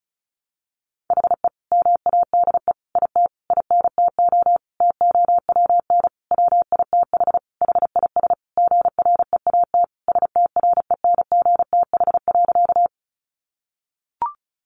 Morse code, heard as a keyed tone, about 34 wpm, 700 Hz. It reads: HE MADE IT INTO TOWN WITH HIS GREAT STRENGTH.
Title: Sentences from Top 1300 Words 35wpm